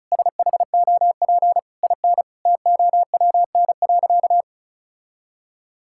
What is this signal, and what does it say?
Morse code, heard as a keyed tone, about 36 wpm, 700 Hz. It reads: SHOP IN TOWN.
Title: Sentences from Top 1600 Words 35wpm